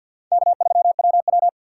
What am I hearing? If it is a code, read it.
Morse code, heard as a keyed tone, about 52 wpm, 700 Hz. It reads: K3WW
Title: USA Call Signs - 4 Characters - 2x Character Spacing - 50wpm